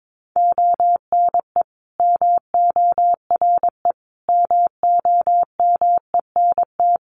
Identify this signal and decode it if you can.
Morse code, heard as a keyed tone, about 22 wpm, 700 Hz. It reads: ONE MORE MOMENT
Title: Sentences from Top 1300 Words 22wpm